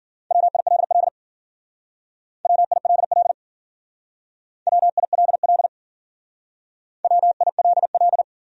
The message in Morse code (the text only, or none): WILL WILL WILL WILL